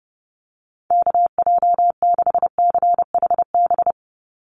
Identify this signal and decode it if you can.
Morse code, heard as a keyed tone, about 30 wpm, 700 Hz. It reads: KJ6CHB
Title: USA Call Signs 30wpm